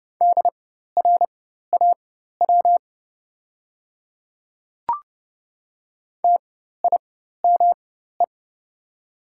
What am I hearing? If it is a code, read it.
Morse code, heard as a keyed tone, about 30 wpm, spaced as for 14 wpm, 700 Hz. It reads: DRAW TIME
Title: 4x Character Spacing - Top 300 Words - 30wpm